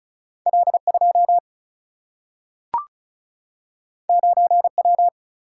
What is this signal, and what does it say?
Morse code, heard as a keyed tone, about 36 wpm, 700 Hz. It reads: L2 9W
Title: Single-Letter Single-Number 35wpm